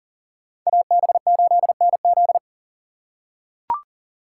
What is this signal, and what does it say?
Morse code, heard as a keyed tone, about 40 wpm, 700 Hz. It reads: AB8NZ
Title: USA Call Signs - 5 Characters - 6x Character Spacing - 40wpm